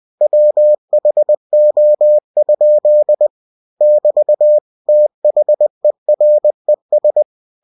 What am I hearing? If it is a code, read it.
Morse code, heard as a keyed tone, about 20 wpm, 600 Hz. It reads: WHO? <BT> THERES
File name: cw-books / winnie-the-pooh / winnie-20wpm-page-0096